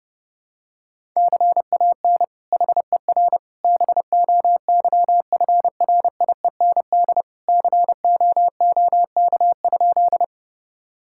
Morse code, heard as a keyed tone, about 32 wpm, 700 Hz. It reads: CAN HER BOYFRIEND COOK?
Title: Sentences from Top 1000 Words - Review Part 5 - 30wpm